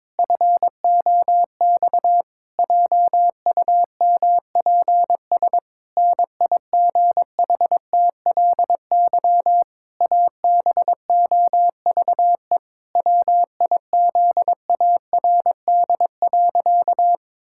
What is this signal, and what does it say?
Morse code, heard as a keyed tone, about 22 wpm, 700 Hz. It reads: FOX JUMPS NIGHTLY ABOVE WIZARD.
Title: Pangram Challenge v2 - 22wpm